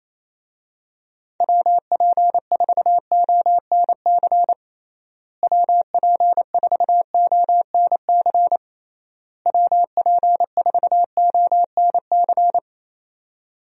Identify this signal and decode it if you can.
Morse code, heard as a keyed tone, about 28 wpm, 700 Hz. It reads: WP4ONC WP4ONC WP4ONC
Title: USA Call Signs - 6 Characters - Repeated 3x - 28wpm